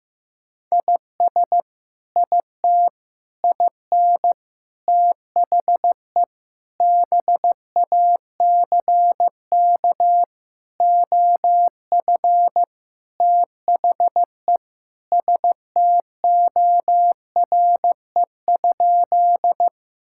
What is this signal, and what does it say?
Morse code, heard as a keyed tone, about 15 wpm, 700 Hz. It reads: IS IT IN THE BACK OF THE STORE?
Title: Sentences from Top 1000 Words - Review Part 5 - 15wpm